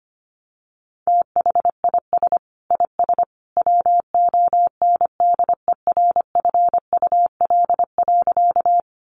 Morse code, heard as a keyed tone, about 25 wpm, 700 Hz. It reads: THIS IS WONDERFUL.
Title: Sentences from Top 1100 Words 25wpm